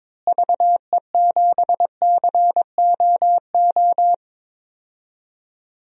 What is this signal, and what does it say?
Morse code, heard as a keyed tone, about 22 wpm, 700 Hz. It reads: VE7COO
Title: DXCC-100 Call Signs - 6 Characters - Repeated 3x - 22wpm